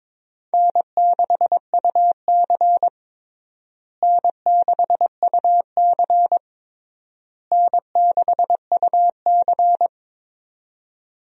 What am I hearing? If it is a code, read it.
Morse code, heard as a keyed tone, about 22 wpm, 700 Hz. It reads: N6UC N6UC N6UC
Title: USA Call Signs - 4 Characters - Repeated 3x - 22wpm